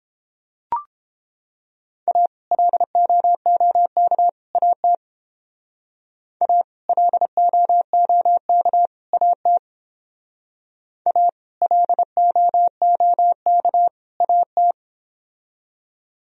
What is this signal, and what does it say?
Morse code, heard as a keyed tone, about 29 wpm, 700 Hz. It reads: A LOOK AT A LOOK AT A LOOK AT